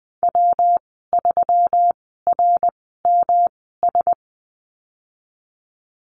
Morse code, heard as a keyed tone, about 20 wpm, spaced as for 15 wpm, 700 Hz. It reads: W3RMS
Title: 154 Lesson 25g - Letter p - Call Signs 20wpm